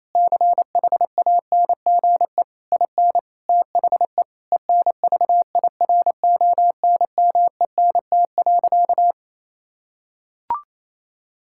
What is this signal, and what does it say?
Morse code, heard as a keyed tone, about 28 wpm, 700 Hz. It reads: CHANGE IN THE ENVIRONMENT.
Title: Sentences from Top 1000 Words - Review Part 6 - 28wpm